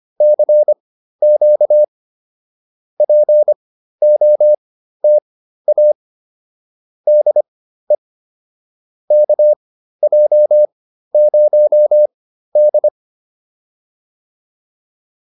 Morse code, heard as a keyed tone, about 25 wpm, spaced as for 13 wpm, 600 Hz. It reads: CQ POTA DE KJ0D